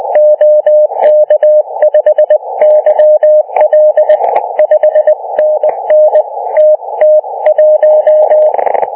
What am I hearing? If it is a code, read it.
Morse code, heard as a keyed tone, about 27 wpm, spaced as for 21 wpm, 660 Hz. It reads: OK5YL 5NN TT1
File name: Archiv / CUC318 / CUC318 záznam OK1DCS